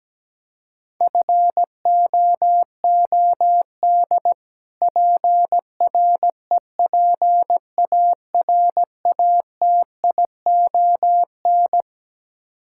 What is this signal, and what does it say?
Morse code, heard as a keyed tone, about 17 wpm, 700 Hz. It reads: FOOD PREPARATION
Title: K1USN SST QSO Practice 17wpm